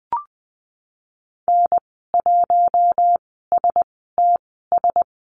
Morse code, heard as a keyed tone, about 20 wpm, spaced as for 14 wpm, 700 Hz. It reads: N1STS
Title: 076 Lesson 13g - Letter L - Call Signs 20wpm